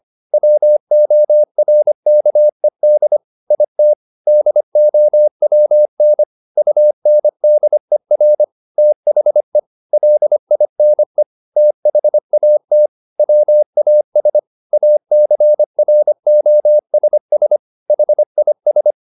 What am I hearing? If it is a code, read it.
Morse code, heard as a keyed tone, about 25 wpm, 600 Hz. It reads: WORKED IT DOWN UNDER THE LINE THAT WAS ACROSS HIS
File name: the old man and the sea part 1 25wpm